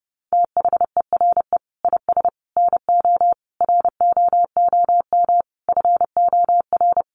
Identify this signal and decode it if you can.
Morse code, heard as a keyed tone, about 28 wpm, 700 Hz. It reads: THERE IS NO ROOM FOR
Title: Sentences from Top 1200 Words 30wpm